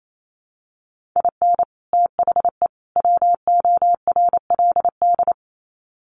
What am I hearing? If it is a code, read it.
Morse code, heard as a keyed tone, about 28 wpm, 700 Hz. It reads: IN THE WORLD
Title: Sets of 3 Words 28wpm